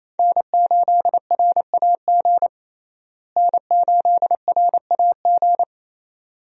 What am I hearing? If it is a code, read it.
Morse code, heard as a keyed tone, about 28 wpm, 700 Hz. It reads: N8RAG N8RAG
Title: USA Call Signs - 5 Characters - Repeated 2x - 28wpm